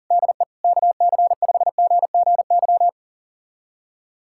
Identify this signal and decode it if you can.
Morse code, heard as a keyed tone, about 40 wpm, 700 Hz. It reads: DE KC5GGY
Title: USA Callsign DE Callsign - 6 Characters - 40wpm